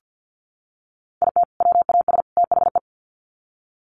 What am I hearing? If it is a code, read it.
Morse code, heard as a keyed tone, about 48 wpm, 700 Hz. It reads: IT WAS THE
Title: Sets of 3 Words 50wpm